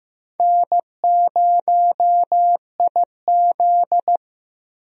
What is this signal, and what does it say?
Morse code, heard as a keyed tone, about 15 wpm, 700 Hz. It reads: N0IZ
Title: USA Call Signs - 4 Characters - Repeated 3x - 15wpm